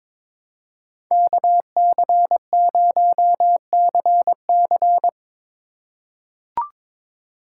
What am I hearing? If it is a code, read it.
Morse code, heard as a keyed tone, about 22 wpm, 700 Hz. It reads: KC0CC
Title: USA Call Signs - 5 Characters - 4x Character Spacing - 22wpm